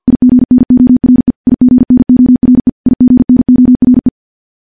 Morse code, heard as a keyed tone, about 50 wpm, 250 Hz. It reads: IGNORE IGNORE IGNORE
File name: morse / morse-training-50WPM